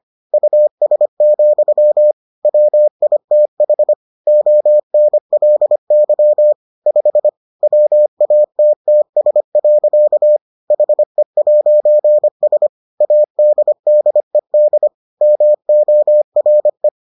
Morse code, heard as a keyed tone, about 25 wpm, 600 Hz. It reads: US, WITH ONLY 5 WATTS. HE'S ADDED MORE